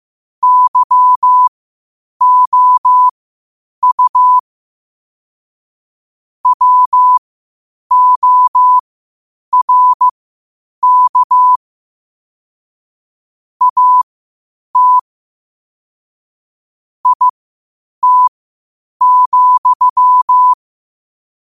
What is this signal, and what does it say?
Morse code, heard as a keyed tone, about 15 wpm, spaced as for 8 wpm, 1000 Hz. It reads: YOU WORK AT IT,
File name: qotdfiles / 8wpm